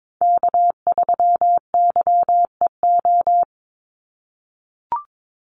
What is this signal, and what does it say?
Morse code, heard as a keyed tone, about 22 wpm, 700 Hz. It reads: K3YEO
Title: Call Signs - 22wpm